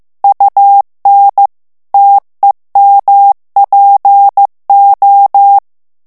Morse code, heard as a keyed tone, about 15 wpm, 800 Hz. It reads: UN TEMPO
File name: ITALIEN texte clair 1 vitesse 14 mots minute